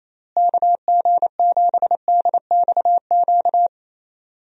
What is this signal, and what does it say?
Morse code, heard as a keyed tone, about 28 wpm, 700 Hz. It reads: KG7DXQ
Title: USA Call Signs - 6 Characters - Repeated 2x - 28wpm